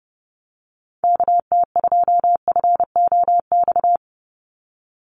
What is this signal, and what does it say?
Morse code, heard as a keyed tone, about 29 wpm, 700 Hz. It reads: KT2FOX
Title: USA Call Signs 30wpm